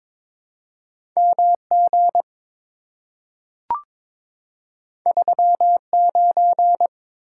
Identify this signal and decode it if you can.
Morse code, heard as a keyed tone, about 22 wpm, 700 Hz. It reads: MG 39
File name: practice / 22wpm / Two Letter-Number Combinations - 22wpm